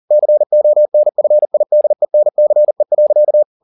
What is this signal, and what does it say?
Morse code, heard as a keyed tone, about 40 wpm, 600 Hz. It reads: CONFIDENCE.
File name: the old man and the sea 2 40wpm